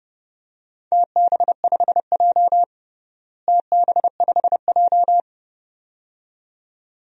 Morse code, heard as a keyed tone, about 30 wpm, 700 Hz. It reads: TB5J TB5J
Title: DXCC-100 Call Signs - 4 Characters - Repeated 2x - 30wpm